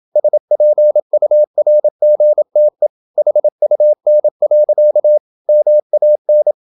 Morse code, heard as a keyed tone, about 27 wpm, 600 Hz. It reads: SPURGTE HUN. MAN
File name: Ferie QRQ 27wpm